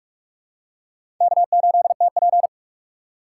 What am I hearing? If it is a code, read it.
Morse code, heard as a keyed tone, about 46 wpm, 700 Hz. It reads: K8TP
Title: USA Call Signs - 4 Characters - 3x Character Spacing - 45wpm